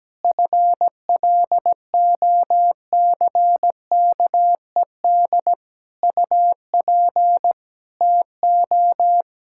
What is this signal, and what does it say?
Morse code, heard as a keyed tone, about 17 wpm, 700 Hz. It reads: FLOCKED UP TO
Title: Pangram Challenge v2 - 17wpm